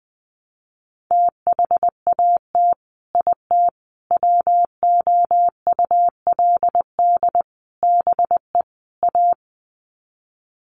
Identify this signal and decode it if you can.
Morse code, heard as a keyed tone, about 20 wpm, 700 Hz. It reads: THAT IT WOULD BE A
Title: Sets of 5 Words Encore2 20wpm